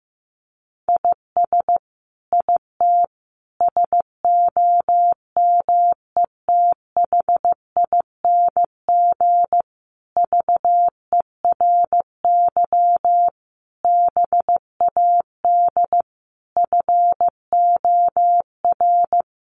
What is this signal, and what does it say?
Morse code, heard as a keyed tone, about 15 wpm, 700 Hz. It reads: IS IT SOMETHING VERY BAD FOR
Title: Sentences from Top 700 Words 15wpm